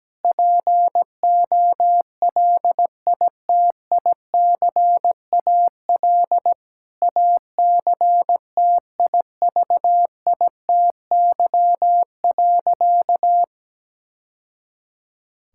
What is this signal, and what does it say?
Morse code, heard as a keyed tone, about 17 wpm, 700 Hz. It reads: POLITICAL ACTIVITY.
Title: Sentences from Top 1000 Words - Review Part 5 - 17wpm